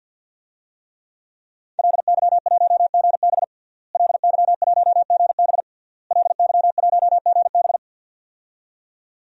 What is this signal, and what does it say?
Morse code, heard as a keyed tone, about 48 wpm, 700 Hz. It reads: LY1GB LY1GB LY1GB
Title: DXCC-100 Call Signs - 5 Characters - Repeated 3x - 50wpm